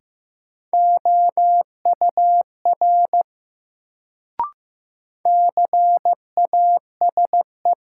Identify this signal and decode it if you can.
Morse code, heard as a keyed tone, about 15 wpm, 700 Hz. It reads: OUR CASE
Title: Sets of 2 Words - New Word Drill - 15wpm